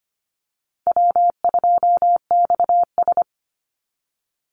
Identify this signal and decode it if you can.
Morse code, heard as a keyed tone, about 25 wpm, 700 Hz. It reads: W2XS
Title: Easy Call Signs - 25wpm